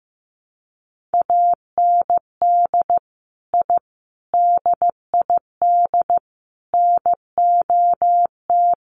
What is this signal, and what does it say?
Morse code, heard as a keyed tone, about 15 wpm, 700 Hz. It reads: AND I DID NOT
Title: Sets of 4 Words 15wpm